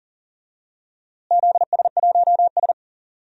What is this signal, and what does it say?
Morse code, heard as a keyed tone, about 40 wpm, 700 Hz. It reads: ZS1S
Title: DXCC-100 Call Signs - 4 Characters - Repeated 2x - 40wpm